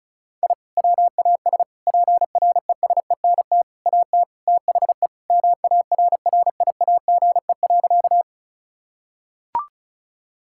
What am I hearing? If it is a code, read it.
Morse code, heard as a keyed tone, about 34 wpm, 700 Hz. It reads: I WAS PRESENT AT THE MARRIAGE.
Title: Sentences from Top 1000 Words - Review Part 5 - 35wpm